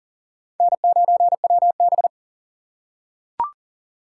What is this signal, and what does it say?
Morse code, heard as a keyed tone, about 40 wpm, 700 Hz. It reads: N9WB
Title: USA Call Signs - 4 Characters - 4x Character Spacing - 40wpm